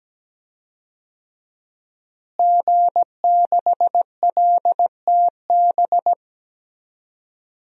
Morse code, heard as a keyed tone, about 17 wpm, 700 Hz. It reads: G6LTB
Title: DXCC-100 Call Signs - 5 Characters - Repeated 2x - 17wpm